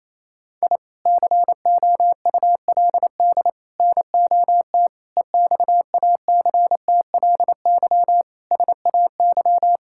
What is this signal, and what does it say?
Morse code, heard as a keyed tone, about 28 wpm, 700 Hz. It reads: I COULD NOT EXACTLY SAY
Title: Sentences from Top 1000 Words - Review Part 6 - 28wpm